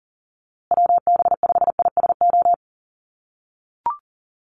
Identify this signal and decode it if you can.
Morse code, heard as a keyed tone, about 40 wpm, 700 Hz. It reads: WB5ISO
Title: USA Call Signs Repeated 3x 40wpm